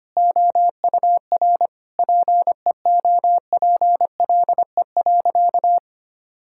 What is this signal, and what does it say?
Morse code, heard as a keyed tone, about 25 wpm, 700 Hz. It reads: OUR PEOPLE.
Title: Sentences from Top 1000 Words - Review Part 6 - 25wpm